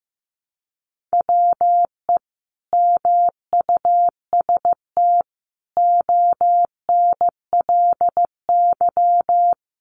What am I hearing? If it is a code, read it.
Morse code, heard as a keyed tone, about 15 wpm, 700 Hz. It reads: WE MUST ONLY
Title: Sentences from Top 1000 Words 15wpm